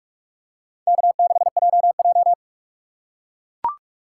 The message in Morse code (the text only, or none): K6JJ